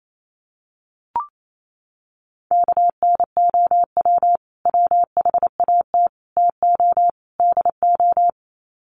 Morse code, heard as a keyed tone, about 28 wpm, 700 Hz. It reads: KNOW WHAT TO DO
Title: Sets of 4 Words 28wpm